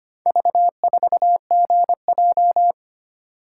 Morse code, heard as a keyed tone, about 25 wpm, 700 Hz. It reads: V4GJ
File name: practice / 25wpm / Four Letter-Number Combinations - 25wpm